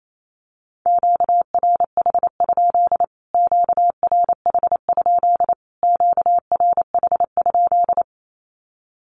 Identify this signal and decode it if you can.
Morse code, heard as a keyed tone, about 28 wpm, 700 Hz. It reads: QRH? QRH? QRH?
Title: Q-Codes Repeated 3x 28wpm